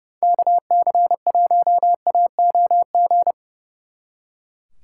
Morse code, heard as a keyed tone, about 40 wpm, 700 Hz. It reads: KC1AOG